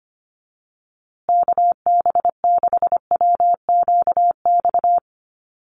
Morse code, heard as a keyed tone, about 25 wpm, 700 Hz. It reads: KB6WQX
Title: USA Call Signs Repeated 3x 25wpm